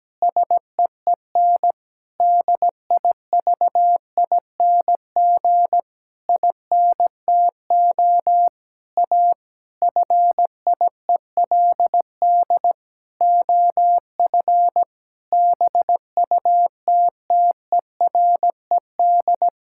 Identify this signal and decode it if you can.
Morse code, heard as a keyed tone, about 17 wpm, 700 Hz. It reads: SEEN DIVING INTO A FIELD OF BUTTERED
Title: Pangram Challenge v2 - 17wpm